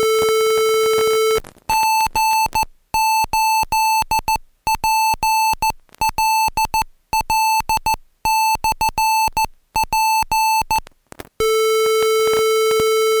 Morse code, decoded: G8PLL/P